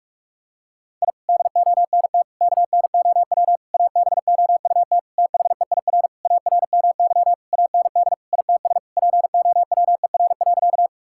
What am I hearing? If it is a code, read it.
Morse code, heard as a keyed tone, about 47 wpm, 700 Hz. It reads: I DONT KNOW ABOUT THEIR ARMY AND ITS POWER.
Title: Sentences from Top 1000 Words - Review Part 5 - 45wpm